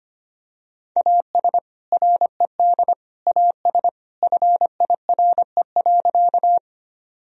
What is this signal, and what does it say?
Morse code, heard as a keyed tone, about 25 wpm, 700 Hz. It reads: AS RED AS FIRE.
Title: Sentences from Top 1000 Words - Review Part 2 - 25wpm